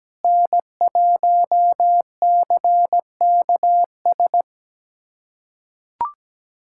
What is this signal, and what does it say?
Morse code, heard as a keyed tone, about 17 wpm, 700 Hz. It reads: N1CKS
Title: USA Callsign DE Callsign - 5 Characters - 17wpm